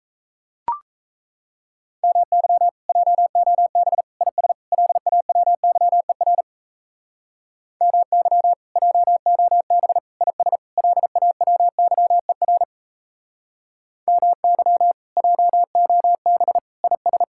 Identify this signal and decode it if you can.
Morse code, heard as a keyed tone, about 35 wpm, 700 Hz. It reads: MY JOB IS LAWYER MY JOB IS LAWYER MY JOB IS